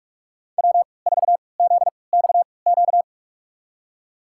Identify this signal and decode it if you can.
Morse code, heard as a keyed tone, about 46 wpm, spaced as for 26 wpm, 700 Hz. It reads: W4ZXQ
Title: USA Call Signs - 5 Characters - 3x Character Spacing - 45wpm